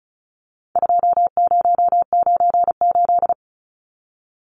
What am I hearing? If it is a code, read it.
Morse code, heard as a keyed tone, about 34 wpm, 700 Hz. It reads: 2098
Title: Numbers - 4-Digit 35wpm